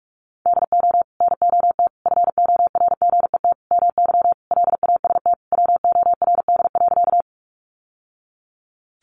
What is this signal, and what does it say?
Morse code, heard as a keyed tone, about 44 wpm, 700 Hz. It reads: DO NOT FORGET MY LAST WORD.
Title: Sentences from Top 1200 Words 45wpm